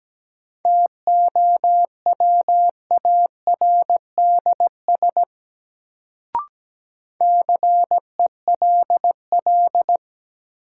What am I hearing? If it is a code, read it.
Morse code, heard as a keyed tone, about 17 wpm, 700 Hz. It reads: TOWARDS CELL
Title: Sentences from Top 1100 Words - New Word Drill Alt - 17wpm